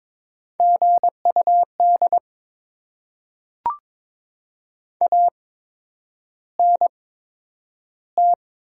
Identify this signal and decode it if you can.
Morse code, heard as a keyed tone, about 22 wpm, spaced as for 9 wpm, 700 Hz. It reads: GUD ANT
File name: Top 20 QSO Elements - 8x Character Spacing - 22wpm